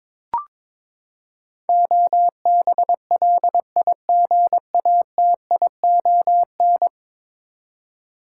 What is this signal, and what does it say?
Morse code, heard as a keyed tone, about 22 wpm, 700 Hz. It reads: OBLIGATION
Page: morse-code-ninja-practice-22wpm